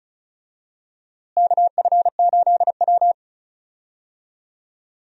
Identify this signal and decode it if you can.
Morse code, heard as a keyed tone, about 36 wpm, 700 Hz. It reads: KF8W
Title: USA Call Signs - 4 Characters - Repeated 3x - 35wpm